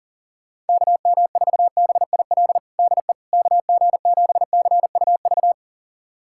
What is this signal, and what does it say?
Morse code, heard as a keyed tone, about 40 wpm, 700 Hz. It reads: KM4BIL DE KG7CUV
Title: USA Callsign DE Callsign - 6 Characters - 40wpm